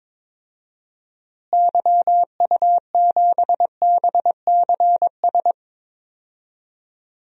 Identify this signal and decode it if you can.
Morse code, heard as a keyed tone, about 22 wpm, 700 Hz. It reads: YU7BCS
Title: DXCC-100 Call Signs - 6 Characters - Repeated 2x - 22wpm